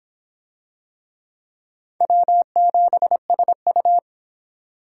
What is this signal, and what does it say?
Morse code, heard as a keyed tone, about 26 wpm, 700 Hz. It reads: W7SU